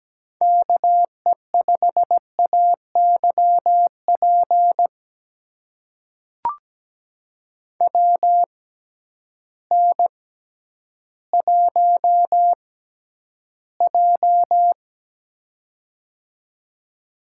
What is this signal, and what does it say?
Morse code, heard as a keyed tone, about 17 wpm, spaced as for 10 wpm, 700 Hz. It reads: KE5AYP WN1J
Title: USA Call Signs - 6x Character Spacing - 17wpm